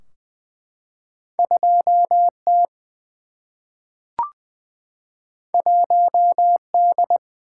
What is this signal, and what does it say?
Morse code, heard as a keyed tone, about 20 wpm, 700 Hz. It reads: 2 T 1 D
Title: Single-Letter Single-Number 20wpm